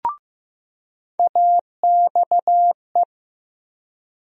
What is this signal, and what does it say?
Morse code, heard as a keyed tone, about 15 wpm, 700 Hz. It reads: AXE
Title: USA Call Sign Postfixes - 3 Letters 15wpm